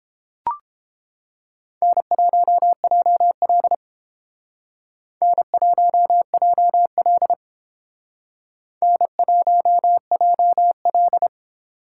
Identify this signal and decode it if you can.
Morse code, heard as a keyed tone, about 30 wpm, 700 Hz. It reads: N1JL N1JL N1JL